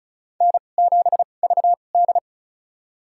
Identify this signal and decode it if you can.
Morse code, heard as a keyed tone, about 36 wpm, spaced as for 25 wpm, 700 Hz. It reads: N7VD
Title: USA Call Signs - 2x Character Spacing - 35wpm